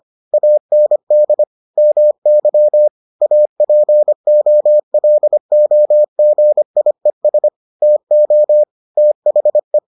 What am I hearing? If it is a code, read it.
Morse code, heard as a keyed tone, about 25 wpm, 600 Hz. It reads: AND MY APOLOGIES TO THE